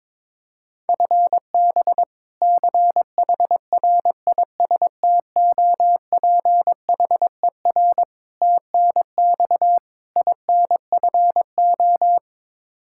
Morse code, heard as a keyed tone, about 22 wpm, 700 Hz. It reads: FB CHRISTOPHER TNX INFO
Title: K1USN SST QSO Practice 22wpm